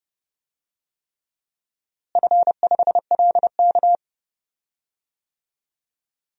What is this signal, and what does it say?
Morse code, heard as a keyed tone, about 29 wpm, 700 Hz. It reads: F5LK
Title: DXCC-100 Call Signs - 4 Characters - Repeated 3x - 30wpm